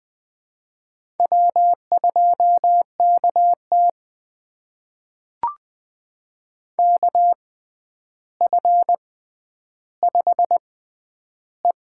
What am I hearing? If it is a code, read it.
Morse code, heard as a keyed tone, about 21 wpm, spaced as for 10 wpm, 700 Hz. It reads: W2KT KF5E
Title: USA Call Signs - 4 Characters - 6x Character Spacing - 20wpm